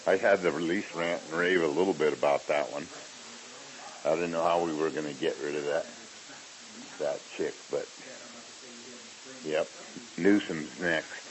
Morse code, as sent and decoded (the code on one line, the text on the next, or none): none